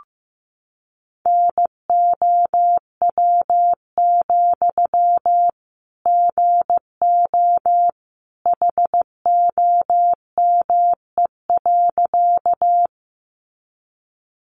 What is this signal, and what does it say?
Morse code, heard as a keyed tone, about 15 wpm, 700 Hz. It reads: NOW, GO HOME.
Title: Sentences from Top 200 Words 15wpm